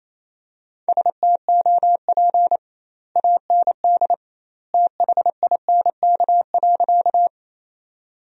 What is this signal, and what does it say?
Morse code, heard as a keyed tone, about 28 wpm, 700 Hz. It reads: STOP AND THINK.
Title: 2x Word-Spacing - Sentences from Top 300 Words - 28wpm